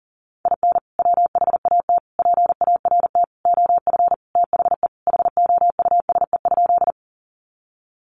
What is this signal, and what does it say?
Morse code, heard as a keyed tone, about 40 wpm, 700 Hz. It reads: IN WHAT PART OF THE HOUSE?
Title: Sentences from Top 200 Words 40wpm